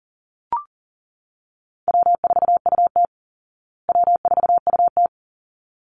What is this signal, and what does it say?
Morse code, heard as a keyed tone, about 37 wpm, 700 Hz. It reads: W4UT W4UT
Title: Easy Call Signs Repeated 2x - 40wpm